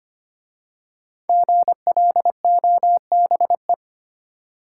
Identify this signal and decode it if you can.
Morse code, heard as a keyed tone, about 25 wpm, 700 Hz. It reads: GLOBE
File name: 5 Letter Words - First 3 Letters Given - 25wpm